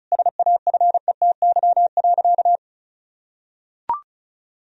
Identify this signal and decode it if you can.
Morse code, heard as a keyed tone, about 36 wpm, 700 Hz. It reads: SAFETY.
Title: Sentences from Top 1000 Words - Review Part 4 - 35wpm